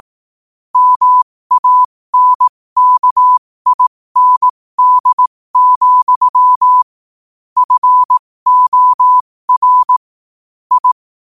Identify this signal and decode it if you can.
Morse code, heard as a keyed tone, about 18 wpm, spaced as for 15 wpm, 1000 Hz. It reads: MANKIND, FOR I
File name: qotdfiles / 15wpm